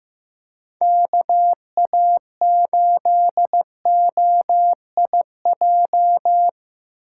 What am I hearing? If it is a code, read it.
Morse code, heard as a keyed tone, about 18 wpm, 700 Hz. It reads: KA8OIJ
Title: Speed Racing - USA 2x3 Call Signs 15wpm